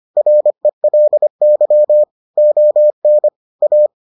Morse code, heard as a keyed tone, about 25 wpm, 600 Hz. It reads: RELY ON A